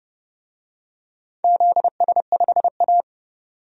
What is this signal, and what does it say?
Morse code, heard as a keyed tone, about 31 wpm, 700 Hz. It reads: ZS5A
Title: DXCC-100 Call Signs - 4 Characters - 30wpm